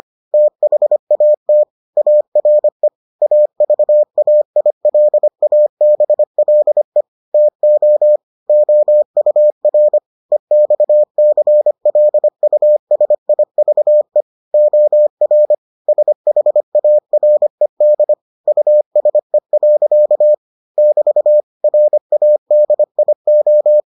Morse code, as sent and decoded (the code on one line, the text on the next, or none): - .... .- - / .- .-. / .- ...- .- .. .-.. .- -... .-.. / - --- / --- ..- .-. / . -..- -.-. .-.. ..- ... .. ...- . / --- .-. / ... .... .- .-. -.. / ..- ... .-.-.- / -...- / .-. .- -.. .. ---
THAT ARE AVAILABLE TO OUR EXCLUSIVE OR SHARED USE. <BT> RADIO